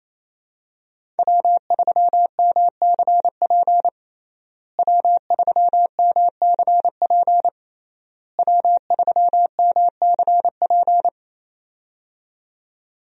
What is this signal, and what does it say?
Morse code, heard as a keyed tone, about 28 wpm, 700 Hz. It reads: W3MCP W3MCP W3MCP
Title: USA Call Signs - 5 Characters - Repeated 3x - 28wpm